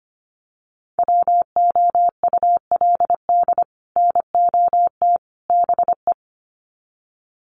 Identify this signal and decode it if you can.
Morse code, heard as a keyed tone, about 25 wpm, 700 Hz. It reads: WOULD NOT BE
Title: Sets of 3 Words 25wpm